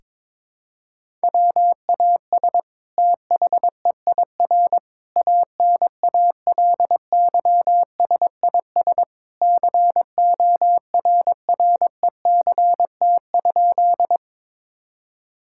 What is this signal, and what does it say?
Morse code, heard as a keyed tone, about 22 wpm, 700 Hz. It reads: WAS THEIR ANALYSIS CORRECT?
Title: Sentences from Top 1000 Words - Review Part 5 - 22wpm